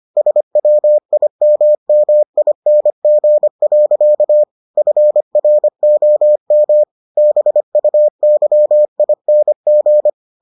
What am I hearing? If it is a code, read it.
Morse code, heard as a keyed tone, about 25 wpm, 600 Hz. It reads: SWIMMING. FROM BUYING